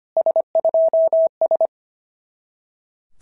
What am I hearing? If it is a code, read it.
Morse code, heard as a keyed tone, about 25 wpm, 660 Hz. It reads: S2S
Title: SOTA QSO Practice 25wpm